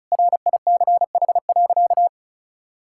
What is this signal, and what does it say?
Morse code, heard as a keyed tone, about 34 wpm, 700 Hz. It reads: RICH.
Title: Sentences from Top 1600 Words 35wpm